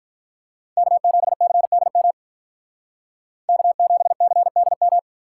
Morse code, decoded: K7CDM K7CDM

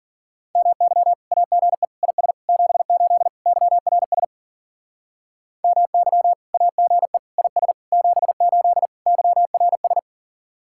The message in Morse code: MY AGE IS 78 YRS MY AGE IS 78 YRS

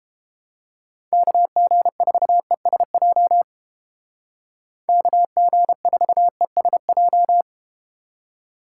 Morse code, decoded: KG4ESJ KG4ESJ

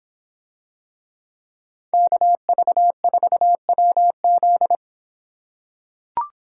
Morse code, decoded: KV4WZ

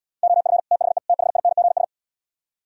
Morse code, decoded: OF THE THINGS

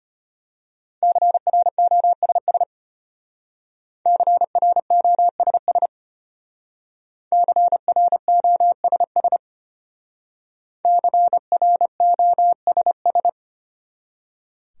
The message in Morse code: CROSS CROSS CROSS CROSS

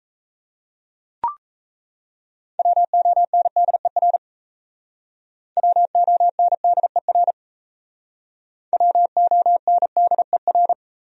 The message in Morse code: WONDER WONDER WONDER

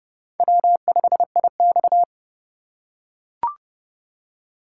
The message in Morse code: W5IX

